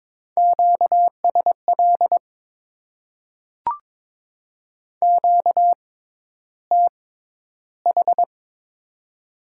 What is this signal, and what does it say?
Morse code, heard as a keyed tone, about 22 wpm, spaced as for 11 wpm, 700 Hz. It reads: QSL QTH
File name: Top 90 QSO Elements - 6x Character Spacing - 22wpm